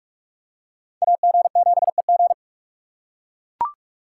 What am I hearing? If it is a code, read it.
Morse code, heard as a keyed tone, about 39 wpm, 700 Hz. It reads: AG7EG